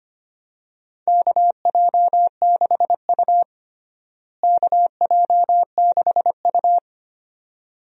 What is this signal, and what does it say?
Morse code, heard as a keyed tone, about 25 wpm, 700 Hz. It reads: KJ6U KJ6U
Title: USA Call Signs - 4 Characters - Repeated 2x - 25wpm